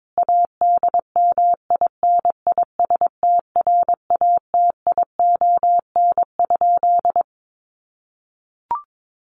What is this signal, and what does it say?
Morse code, heard as a keyed tone, about 22 wpm, 700 Hz. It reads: ADMINISTRATION?
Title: Sentences from Top 1300 Words 22wpm